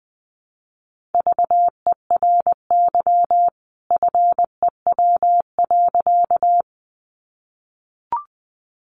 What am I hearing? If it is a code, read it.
Morse code, heard as a keyed tone, about 20 wpm, 700 Hz. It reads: VERY FEW.
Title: A Sound Of Thunder 20wpm